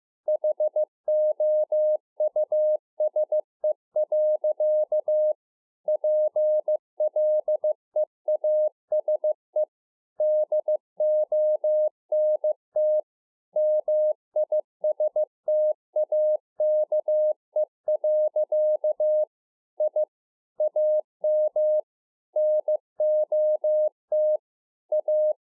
Morse code, heard as a keyed tone, about 15 wpm, 610 Hz. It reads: HOUSE. PLEASE DONT MISTAKE. I AM NOT A